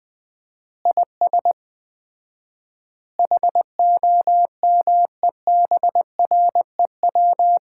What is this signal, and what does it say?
Morse code, heard as a keyed tone, about 20 wpm, 700 Hz. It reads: IS HOMEBREW
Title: Common QSO Phrases - 4x Word-Spacing - 20wpm